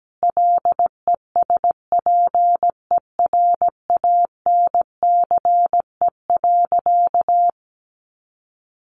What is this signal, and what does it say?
Morse code, heard as a keyed tone, about 17 wpm, 700 Hz. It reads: LESPERANCE.
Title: A Sound Of Thunder 17wpm